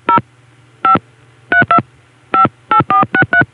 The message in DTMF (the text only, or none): 0 5 32 58463